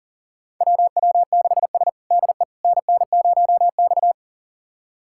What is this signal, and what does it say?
Morse code, heard as a keyed tone, about 40 wpm, 700 Hz. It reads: WW6S DE NN0X
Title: USA Callsign DE Callsign - 4 Characters - 40wpm